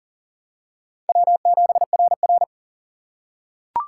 Morse code, heard as a keyed tone, about 40 wpm, 700 Hz. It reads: W7RR
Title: USA Call Signs - 4 Characters - 6x Character Spacing - 40wpm